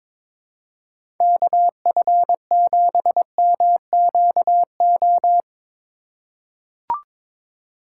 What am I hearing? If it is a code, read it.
Morse code, heard as a keyed tone, about 22 wpm, 700 Hz. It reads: KF7MQO
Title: USA Call Signs - 6 Characters - 3x Character Spacing - 22wpm